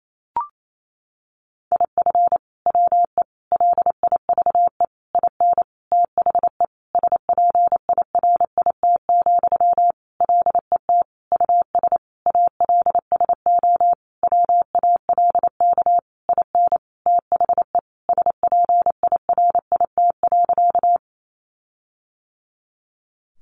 Morse code, heard as a keyed tone, about 28 wpm, 700 Hz. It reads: IF WE LIVE IN THE SPIRIT, LET US ALSO WALK IN THE SPIRIT.
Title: Sentences from Top 1400 Words 28wpm